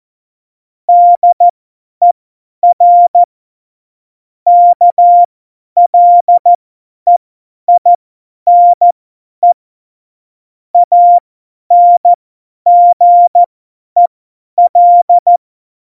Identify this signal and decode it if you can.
Morse code, heard as a keyed tone, about 14 wpm, spaced as for 10 wpm, 700 Hz. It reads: DER KLEINE ANGEL
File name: Morsekurs / Witze / Witz 24. Oktober 17, Tempo 50